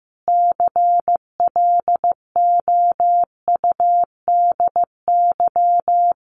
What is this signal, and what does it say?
Morse code, heard as a keyed tone, about 15 wpm, 700 Hz. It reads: CLOUDY
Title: Common QSO Phrases 15wpm